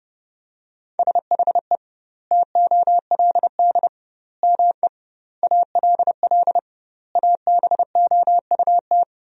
Morse code, heard as a keyed tone, about 30 wpm, 700 Hz. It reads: SHE TOLD ME ALL ABOUT
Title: 2x Word-Spacing - Sentences from Top 400 Words - 30wpm